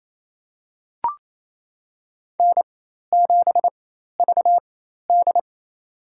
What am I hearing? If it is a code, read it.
Morse code, heard as a keyed tone, about 28 wpm, spaced as for 13 wpm, 700 Hz. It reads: N7VD
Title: USA Call Signs - 4 Characters - 4x Character Spacing - 28wpm